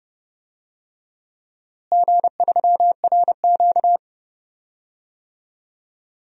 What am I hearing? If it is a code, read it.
Morse code, heard as a keyed tone, about 32 wpm, 700 Hz. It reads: G3RQ
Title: DXCC-100 Call Signs - 4 Characters - Repeated 3x - 30wpm